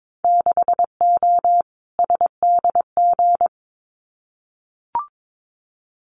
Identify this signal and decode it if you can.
Morse code, heard as a keyed tone, about 22 wpm, 700 Hz. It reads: 6O SDG
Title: Winter Field Day Exchanges Encore 22wpm